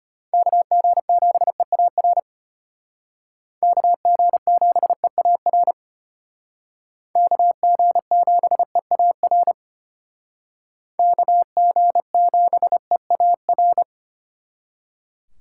Morse code, heard as a keyed tone, about 32 wpm, 700 Hz. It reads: KG7EAR KG7EAR KG7EAR KG7EAR